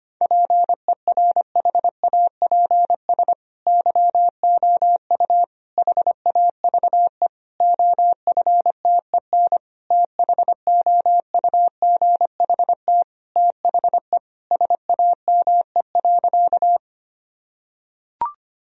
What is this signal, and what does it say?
Morse code, heard as a keyed tone, about 25 wpm, 700 Hz. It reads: PERHAPS YOU HAVE OFTEN THOUGHT THE SAME.
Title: Sentences from Top 1000 Words - Review Part 1 - 25wpm